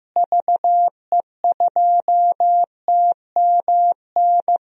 Morse code, heard as a keyed tone, about 15 wpm, 700 Hz. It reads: VE2TMN
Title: DXCC-100 Call Signs - 6 Characters - Repeated 2x - 15wpm